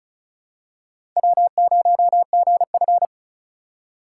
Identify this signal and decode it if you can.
Morse code, heard as a keyed tone, about 36 wpm, 700 Hz. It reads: W0GF
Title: USA Call Signs - 4 Characters - 3x Character Spacing - 35wpm